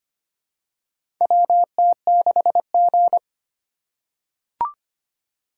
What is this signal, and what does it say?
Morse code, heard as a keyed tone, about 25 wpm, 700 Hz. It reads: WT6G
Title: USA Call Signs - 4 Characters - 2x Character Spacing - 25wpm